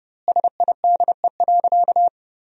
Morse code, heard as a keyed tone, about 32 wpm, 700 Hz. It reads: SIDE.
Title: 2x Word-Spacing - Sentences from Top 400 Words - 30wpm